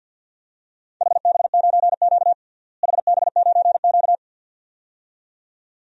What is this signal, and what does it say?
Morse code, heard as a keyed tone, about 47 wpm, 700 Hz. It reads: HB9Q HB9Q
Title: DXCC-100 Call Signs - 4 Characters - Repeated 2x - 50wpm